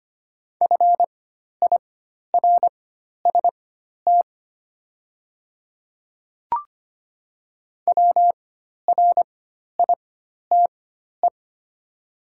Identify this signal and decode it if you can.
Morse code, heard as a keyed tone, about 25 wpm, spaced as for 12 wpm, 700 Hz. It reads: FIRST WRITE
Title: 4x Character Spacing - Top 100 Words - 25wpm